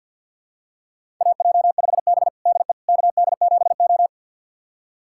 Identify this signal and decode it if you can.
Morse code, heard as a keyed tone, about 52 wpm, 700 Hz. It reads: AJ5B DE KD7O
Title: USA Callsign DE Callsign - 4 Characters - 50wpm